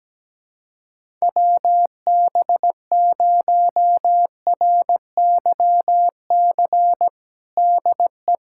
Morse code, heard as a keyed tone, about 17 wpm, 700 Hz. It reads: WB0RYC DE
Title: USA Callsign DE Callsign 17wpm